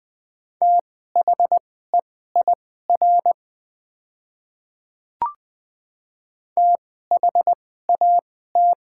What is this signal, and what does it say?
Morse code, heard as a keyed tone, about 20 wpm, spaced as for 14 wpm, 700 Hz. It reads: THEIR THAT
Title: 2x Character Spacing - Top 500 Words - 20wpm